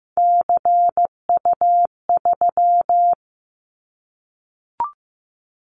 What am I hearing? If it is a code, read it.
Morse code, heard as a keyed tone, about 15 wpm, 700 Hz. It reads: CU3
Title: Callsign Prefixes for Top 50 DXCC - 15wpm